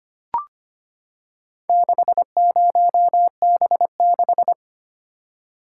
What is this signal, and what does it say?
Morse code, heard as a keyed tone, about 25 wpm, 700 Hz. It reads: 60B6
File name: practice / 25wpm / Four Letter-Number Combinations - 25wpm